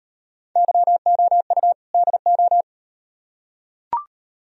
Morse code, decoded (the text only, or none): YOU DO